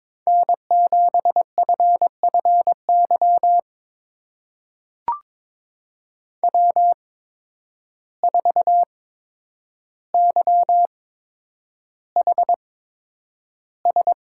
Morse code, heard as a keyed tone, about 22 wpm, spaced as for 9 wpm, 700 Hz. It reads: N7FFY W4YHS
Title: USA Call Signs - 5 Characters - 8x Character Spacing - 22wpm